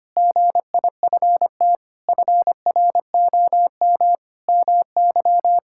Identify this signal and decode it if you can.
Morse code, heard as a keyed tone, about 25 wpm, 700 Hz. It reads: GIFT FROM MY
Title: Sentences from Top 1000 Words - Review Part 6 - 25wpm